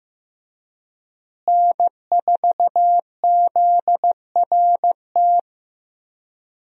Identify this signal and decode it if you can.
Morse code, heard as a keyed tone, about 15 wpm, 700 Hz. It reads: N4ZRT
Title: USA Call Signs - 5 Characters - Repeated 3x - 15wpm